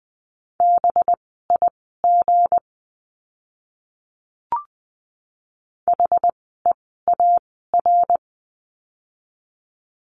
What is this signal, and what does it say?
Morse code, heard as a keyed tone, about 20 wpm, spaced as for 14 wpm, 700 Hz. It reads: BIG HEAR
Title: 206 Lesson 33f - Letter j - Words 20wpm